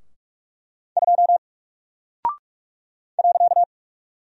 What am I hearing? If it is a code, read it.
Morse code, heard as a keyed tone, about 44 wpm, 700 Hz. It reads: 2 .
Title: Single Character (Letter:Number:Punctuation:Prosign) 45wpm